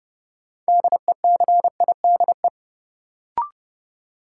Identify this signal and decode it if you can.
Morse code, heard as a keyed tone, about 30 wpm, 700 Hz. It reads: DECIDE
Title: Sentences from Top 500 Words - New Word Drill - 30wpm